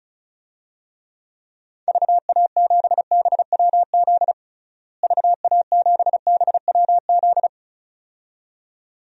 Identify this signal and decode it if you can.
Morse code, heard as a keyed tone, about 36 wpm, 700 Hz. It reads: VA7BWZ VA7BWZ
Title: DXCC-100 Call Signs - 6 Characters - Repeated 2x - 35wpm